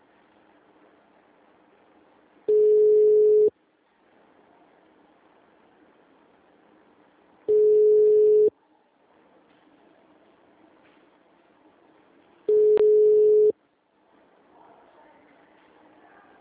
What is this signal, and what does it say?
Ringback tone (Europe, 425 Hz).